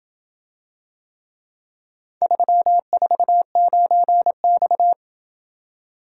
Morse code, heard as a keyed tone, about 27 wpm, 700 Hz. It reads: 349X